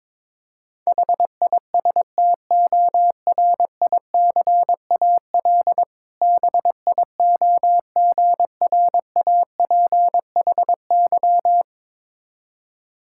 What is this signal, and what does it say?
Morse code, heard as a keyed tone, about 22 wpm, 700 Hz. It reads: HISTORICAL BIOGRAPHY
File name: Difficult Words with Hints - 22wpm